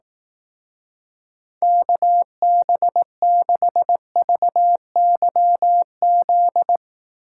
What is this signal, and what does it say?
Morse code, heard as a keyed tone, about 18 wpm, 700 Hz. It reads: KB6VYZ
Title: Speed Racing - USA 2x3 Call Signs 15wpm